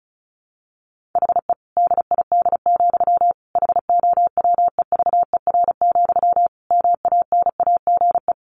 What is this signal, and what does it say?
Morse code, heard as a keyed tone, about 34 wpm, 700 Hz. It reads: HE DID, HOWEVER, MANAGE
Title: Sentences from Top 1400 Words 35wpm